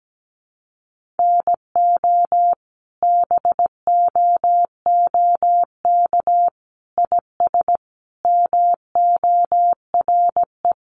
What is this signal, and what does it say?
Morse code, heard as a keyed tone, about 17 wpm, 700 Hz. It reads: NO BOOK IS MORE
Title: Sentences from Top 1200 Words 17wpm